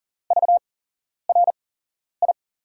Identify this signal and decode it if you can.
Morse code, heard as a keyed tone, about 40 wpm, spaced as for 11 wpm, 700 Hz. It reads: VRI
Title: USA Call Signs - 6 Characters - 8x Character Spacing - 40wpm